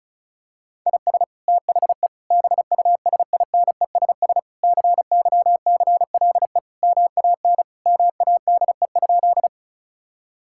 Morse code, heard as a keyed tone, about 36 wpm, 700 Hz. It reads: IS THE BUSINESS CYCLE MAN MADE?
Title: Sentences from Top 1000 Words - Review Part 6 - 35wpm